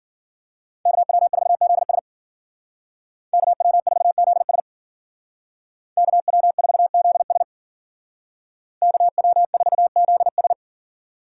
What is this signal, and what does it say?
Morse code, heard as a keyed tone, about 52 wpm, 700 Hz. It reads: KW4ZS KW4ZS KW4ZS KW4ZS